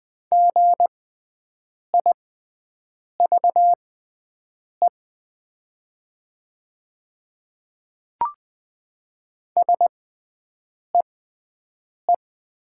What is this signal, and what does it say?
Morse code, heard as a keyed tone, about 20 wpm, spaced as for 7 wpm, 700 Hz. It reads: GIVE SEE
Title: 6x Character Spacing - Top 200 Words - 20wpm